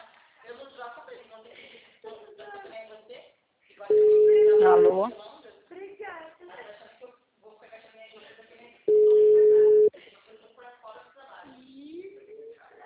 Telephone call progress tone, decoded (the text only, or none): ringback tone